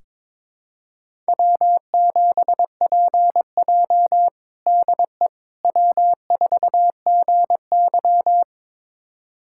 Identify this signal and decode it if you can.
Morse code, heard as a keyed tone, about 21 wpm, 700 Hz. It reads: W7PJ DE W4GY
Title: USA Callsign DE Callsign - 4 Characters - 22wpm